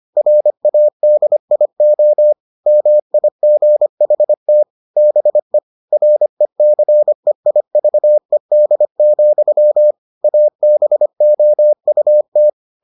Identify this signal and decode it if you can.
Morse code, heard as a keyed tone, about 25 wpm, 600 Hz. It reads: RADIO MIGHT BE RECEIVED, ABOUT